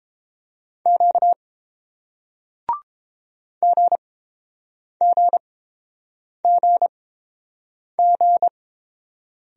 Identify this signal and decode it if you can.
Morse code, heard as a keyed tone, about 30 wpm, spaced as for 7 wpm, 700 Hz. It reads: Q GGGG